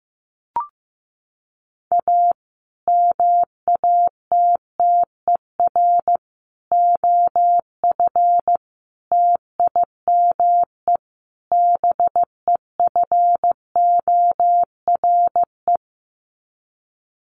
Morse code, heard as a keyed tone, about 15 wpm, 700 Hz. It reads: A MATTER OF TIME BEFORE
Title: Sets of 5 Words Encore 15wpm